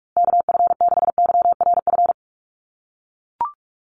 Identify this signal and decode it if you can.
Morse code, heard as a keyed tone, about 44 wpm, 700 Hz. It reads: KF6YRF
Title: USA Call Signs Repeated 2x 45wpm